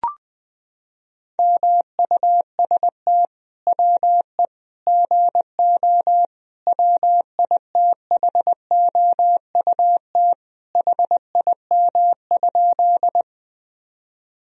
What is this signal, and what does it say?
Morse code, heard as a keyed tone, about 21 wpm, 700 Hz. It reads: MUST WE GO WITHOUT HIM?
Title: Sentences from Top 1000 Words - Review Part 3 - 20wpm